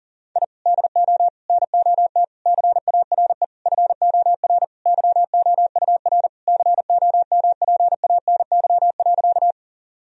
Morse code, heard as a keyed tone, about 40 wpm, 700 Hz. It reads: I DO NOT CARE FOR YOUR COMPANY.
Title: Sentences from Top 1000 Words - Review Part 3 - 40wpm